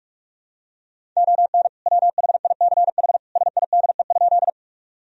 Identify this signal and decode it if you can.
Morse code, heard as a keyed tone, about 46 wpm, 700 Hz. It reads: ON WHICH SIDE?
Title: Sentences from Top 1000 Words - Review Part 1 - 45wpm